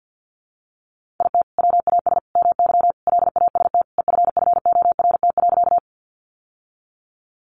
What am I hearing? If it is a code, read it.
Morse code, heard as a keyed tone, about 50 wpm, 700 Hz. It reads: IT WAS MY LAST EFFORT.
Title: Sentences from Top 1200 Words 50wpm